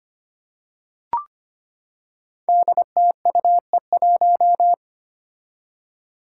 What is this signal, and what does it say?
Morse code, heard as a keyed tone, about 25 wpm, 700 Hz. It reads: DTUE1
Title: Five Challenging Letter-Number Combinations 25wpm